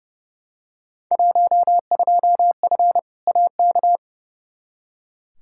Morse code, heard as a keyed tone, about 32 wpm, 700 Hz. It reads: 12F AK
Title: Field Day Exchanges Encore - 30wpm